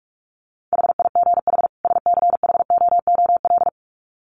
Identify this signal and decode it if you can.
Morse code, heard as a keyed tone, about 44 wpm, 700 Hz. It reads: HIGH SCHOOL
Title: Sets of 2 Words 45wpm